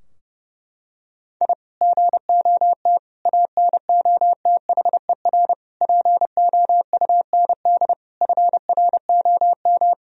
Morse code, heard as a keyed tone, about 32 wpm, 700 Hz. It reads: I GOT ANOTHER POUND FROM
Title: Sentences from Top 1000 Words - Review Part 2 - 30wpm